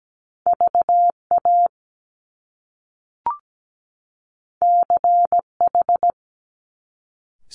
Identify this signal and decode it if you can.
Morse code, heard as a keyed tone, about 17 wpm, 700 Hz. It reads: VA CH
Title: Top Two Letter Combinations in Words 17wpm